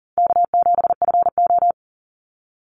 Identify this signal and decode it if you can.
Morse code, heard as a keyed tone, about 42 wpm, 700 Hz. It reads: K7FO
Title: NA SOTA Chaser Callsigns 40wpm